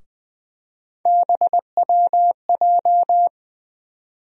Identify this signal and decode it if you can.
Morse code, heard as a keyed tone, about 21 wpm, 700 Hz. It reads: BWJ
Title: USA Call Sign Postfixes - 3 Letters 20wpm